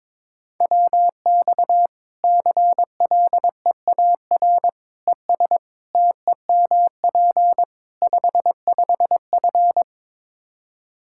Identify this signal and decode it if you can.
Morse code, heard as a keyed tone, about 22 wpm, 700 Hz. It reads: WX CLEAR ES TEMP 55F
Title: Practice QSO - Exchange 2 - Info - 22wpm